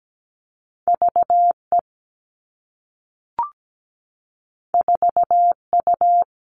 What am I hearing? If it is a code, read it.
Morse code, heard as a keyed tone, about 17 wpm, 700 Hz. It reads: V E 4 U
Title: Two Challenging Letter-Number Combinations 17wpm